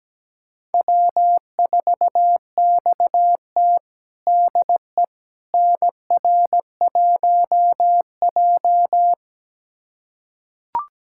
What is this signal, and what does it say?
Morse code, heard as a keyed tone, about 17 wpm, 700 Hz. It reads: W4XT DE NR1J
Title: USA Callsign DE Callsign - 4 Characters - 17wpm